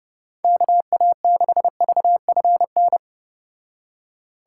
Morse code, heard as a keyed tone, about 31 wpm, 700 Hz. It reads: KA6VFN